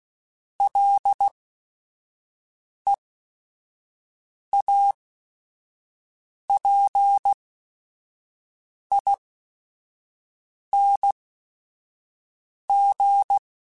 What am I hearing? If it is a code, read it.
Morse code, heard as a keyed tone, about 16 wpm, spaced as for 5 wpm, 800 Hz. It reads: LEAPING
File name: Special INGs / Special INGs-mp3s / Sngl Ing @5